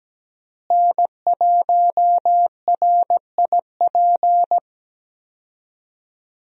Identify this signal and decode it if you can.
Morse code, heard as a keyed tone, about 17 wpm, 700 Hz. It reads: N1RIP
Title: USA Call Signs - 5 Characters - Repeated 3x - 17wpm